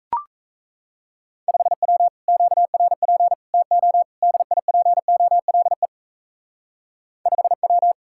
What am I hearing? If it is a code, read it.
Morse code, heard as a keyed tone, about 35 wpm, 700 Hz. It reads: 5W QRP TO DIPOLE 5W
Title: Speed-Racing Common QSO Phrases - 28wpm